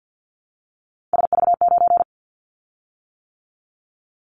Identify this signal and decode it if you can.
Morse code, heard as a keyed tone, about 50 wpm, 700 Hz. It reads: SV9
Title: Callsign Prefixes for Top 100 DXCC - 50wpm